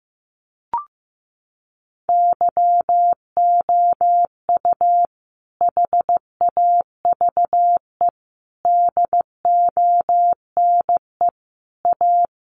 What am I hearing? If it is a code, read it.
Morse code, heard as a keyed tone, about 14 wpm, 700 Hz. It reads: YOU HAVE DONE A
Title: Sentences from Top 700 Words 15wpm